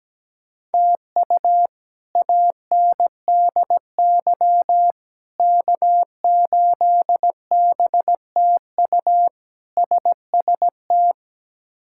Morse code, heard as a keyed tone, about 17 wpm, 700 Hz. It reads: TU ANDY K8BTU SST
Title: K1USN SST QSO Practice - 17wpm